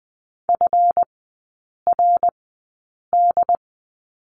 Morse code, heard as a keyed tone, about 20 wpm, spaced as for 8 wpm, 690 Hz. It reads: FRD
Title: 166 Lesson 27e - Number 7 - Three Characters 20wpm